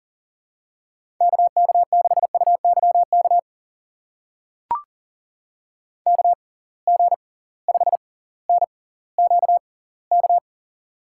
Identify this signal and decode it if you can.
Morse code, heard as a keyed tone, about 40 wpm, spaced as for 20 wpm, 700 Hz. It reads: KK6UYK KG5NQK
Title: USA Call Signs - 6x Character Spacing - 40wpm